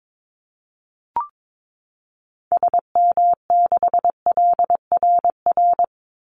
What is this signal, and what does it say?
Morse code, heard as a keyed tone, about 22 wpm, 700 Hz. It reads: SM6LRR
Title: Call Signs Repeated 3x - 22wpm